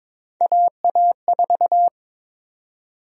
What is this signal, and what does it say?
Morse code, heard as a keyed tone, about 22 wpm, 700 Hz. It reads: AA4
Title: USA Call Sign Prefixes - Repeated 2x - 22wpm